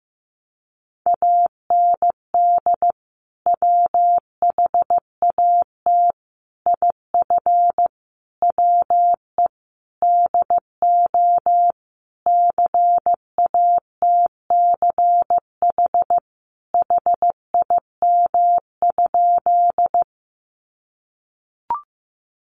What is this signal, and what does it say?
Morse code, heard as a keyed tone, about 15 wpm, 700 Hz. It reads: AND WHAT IF WE DO CATCH HIM?
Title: Sentences from Top 1500 Words 15wpm